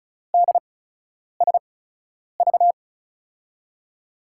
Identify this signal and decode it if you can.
Morse code, heard as a keyed tone, about 36 wpm, spaced as for 10 wpm, 700 Hz. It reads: DSV
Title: USA Call Signs - 5 Characters - 8x Character Spacing - 35wpm